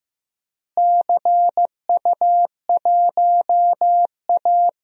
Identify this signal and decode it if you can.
Morse code, heard as a keyed tone, about 15 wpm, 700 Hz. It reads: CU1A
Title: DXCC-100 Call Signs - 4 Characters - Repeated 2x - 15wpm